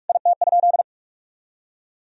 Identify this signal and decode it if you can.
Morse code, heard as a keyed tone, about 46 wpm, 700 Hz. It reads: IT?
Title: Sentences from Top 1000 Words - Review Part 3 - 45wpm